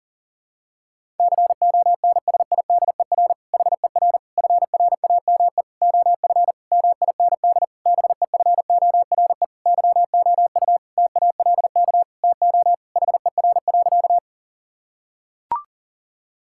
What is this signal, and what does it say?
Morse code, heard as a keyed tone, about 40 wpm, 700 Hz. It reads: CONSIDER HER FRAME OF MIND BEFORE YOU TALK TO HER.
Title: Sentences from Top 1000 Words - Review Part 6 - 40wpm